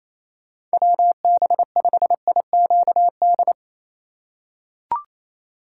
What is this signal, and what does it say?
Morse code, heard as a keyed tone, about 28 wpm, 700 Hz. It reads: WB5IQD
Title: USA Call Signs - 8x Character Spacing - 28wpm